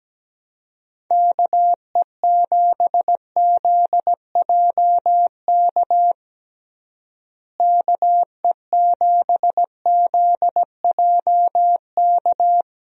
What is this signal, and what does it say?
Morse code, heard as a keyed tone, about 17 wpm, 700 Hz. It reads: KE7ZJK KE7ZJK
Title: USA Call Signs - 6 Characters - Repeated 2x - 17wpm